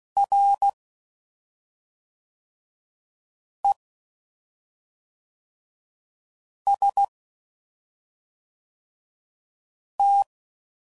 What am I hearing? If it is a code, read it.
Morse code, heard as a keyed tone, about 16 wpm, 800 Hz. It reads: R E S T
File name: Basic Letter Groups By Blocks / Groups@5 / AWUNDG@5 f15